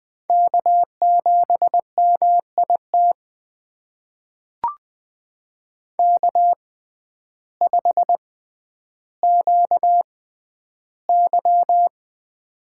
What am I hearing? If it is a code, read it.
Morse code, heard as a keyed tone, about 20 wpm, spaced as for 11 wpm, 700 Hz. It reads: K7MIT K5QY
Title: USA Call Signs - 6x Character Spacing - 20wpm